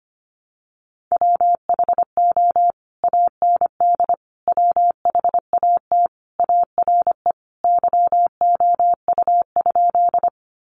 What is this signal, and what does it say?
Morse code, heard as a keyed tone, about 25 wpm, 700 Hz. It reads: WHO AND WHAT ARE YOU?
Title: Sentences from Top 100 Words 25wpm